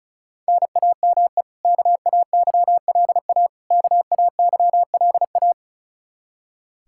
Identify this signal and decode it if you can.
Morse code, heard as a keyed tone, about 34 wpm, 700 Hz. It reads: NAME KAYLA KAYLA
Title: Practice QSO - Exchange 1 - Beginning - 35wpm